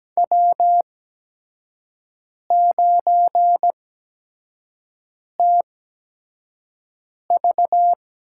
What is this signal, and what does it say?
Morse code, heard as a keyed tone, about 17 wpm, spaced as for 5 wpm, 700 Hz. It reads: W9TV
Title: USA Call Signs - 4 Characters - 8x Character Spacing - 17wpm